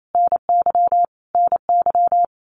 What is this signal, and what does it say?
Morse code, heard as a keyed tone, about 28 wpm, 700 Hz. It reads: NY NY
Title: Common QSO Phrases Encore 28wpm